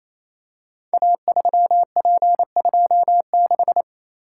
Morse code, heard as a keyed tone, about 29 wpm, 700 Hz. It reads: A3P26
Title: Code Groups - 5 Letter:Numbers - 28wpm